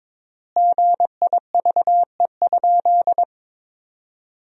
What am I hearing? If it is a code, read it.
Morse code, heard as a keyed tone, about 22 wpm, 700 Hz. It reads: GIVE?
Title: 3x Word-Spacing - Sentences from Top 200 Words - 22wpm